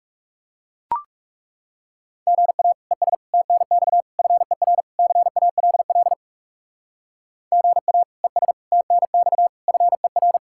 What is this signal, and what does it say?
Morse code, heard as a keyed tone, about 35 wpm, 700 Hz. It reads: GA ES TNX FER CALL GA ES TNX FER
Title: Speed-Racing Common QSO Phrases - 30wpm